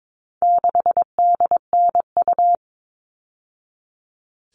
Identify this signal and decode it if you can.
Morse code, heard as a keyed tone, about 22 wpm, 700 Hz. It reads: 6DNU
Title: All Cut Numbers - 4-Digit 22wpm